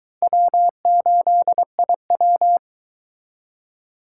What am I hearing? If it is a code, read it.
Morse code, heard as a keyed tone, about 23 wpm, 700 Hz. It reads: W8IW